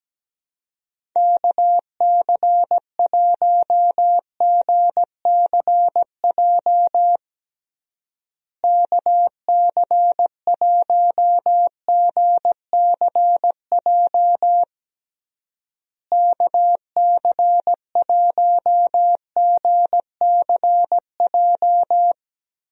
Morse code, heard as a keyed tone, about 17 wpm, 700 Hz. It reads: KC1GCJ KC1GCJ KC1GCJ
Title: USA Call Signs - 6 Characters - Repeated 3x - 17wpm